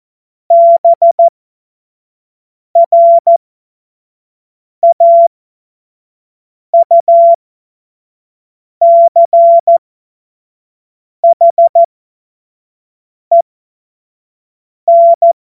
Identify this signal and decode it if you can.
Morse code, heard as a keyed tone, about 14 wpm, spaced as for 5 wpm, 680 Hz. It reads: BRAUCHEN
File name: Morsekurs / Witze / Witz 25. Januar 2022, 25 F70